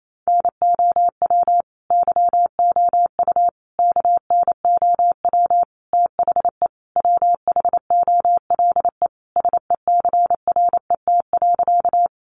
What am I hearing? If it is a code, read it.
Morse code, heard as a keyed tone, about 28 wpm, 700 Hz. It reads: NOW YOU KNOW THE WHOLE SECRET.
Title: Sentences from Top 1200 Words 28wpm